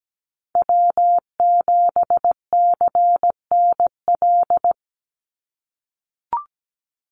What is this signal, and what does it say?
Morse code, heard as a keyed tone, about 17 wpm, 700 Hz. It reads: W7CNL
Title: NA SOTA Chaser Callsigns 17wpm